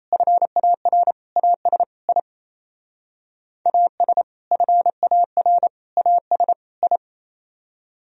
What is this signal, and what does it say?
Morse code, heard as a keyed tone, about 35 wpm, 700 Hz. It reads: FAR AS I AS FAR AS I